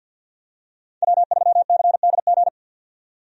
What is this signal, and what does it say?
Morse code, heard as a keyed tone, about 48 wpm, 700 Hz. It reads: W3CDG
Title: USA Call Signs - 5 Characters - 50wpm